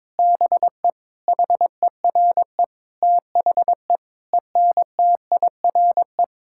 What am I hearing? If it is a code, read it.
Morse code, heard as a keyed tone, about 21 wpm, 700 Hz. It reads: BE HERE THE ENTIRE
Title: Sentences from Top 1000 Words - Review Part 5 - 22wpm